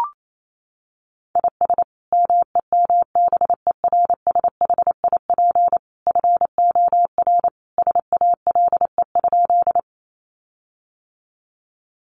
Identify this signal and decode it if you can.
Morse code, heard as a keyed tone, about 28 wpm, 700 Hz. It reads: IS MEMBERSHIP FOR SALE?
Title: Sentences from Top 1400 Words 28wpm